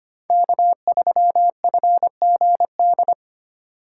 Morse code, heard as a keyed tone, about 25 wpm, 700 Hz. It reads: K3FGD
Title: USA Call Signs - 5 Characters - 3x Character Spacing - 25wpm